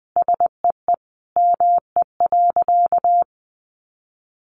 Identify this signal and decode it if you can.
Morse code, heard as a keyed tone, about 20 wpm, 700 Hz. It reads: SEE ME.
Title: Sentences from Top 600 Words 20wpm